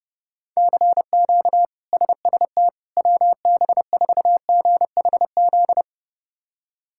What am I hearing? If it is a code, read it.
Morse code, heard as a keyed tone, about 31 wpm, 700 Hz. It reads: CQ SST WB4GHZ
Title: K1USN SST QSO Practice - 30wpm